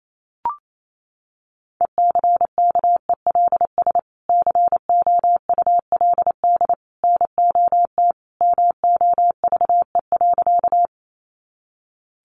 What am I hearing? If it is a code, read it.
Morse code, heard as a keyed tone, about 28 wpm, 700 Hz. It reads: ECKELS COULD NOT MOVE.
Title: A Sound Of Thunder 28wpm